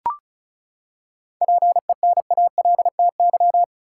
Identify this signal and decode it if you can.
Morse code, heard as a keyed tone, about 34 wpm, 700 Hz. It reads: PENALTY
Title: Seven Letter Words - 35wpm